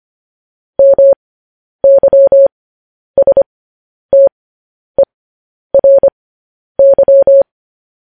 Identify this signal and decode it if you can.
Morse code, heard as a keyed tone, about 25 wpm, spaced as for 10 wpm, 550 Hz. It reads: MYSTERY
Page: news headlines in morse code at 10 wpm